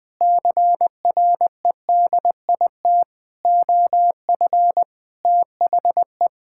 Morse code, heard as a keyed tone, about 20 wpm, 700 Hz. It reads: CREDIT OF THE
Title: Sentences with 7-Word Phrases - 20wpm